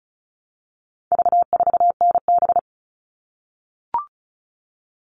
Morse code, heard as a keyed tone, about 34 wpm, 700 Hz. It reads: V4NB
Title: All Cut Numbers - 4-Digit 35wpm